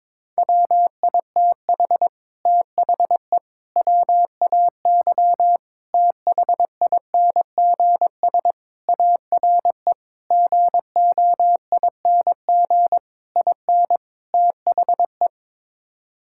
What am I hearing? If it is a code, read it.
Morse code, heard as a keyed tone, about 22 wpm, 700 Hz. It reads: WITH THE WAY THINGS ARE GOING IN THE
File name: practice / 22wpm / Sets of 8 Words - 22wpm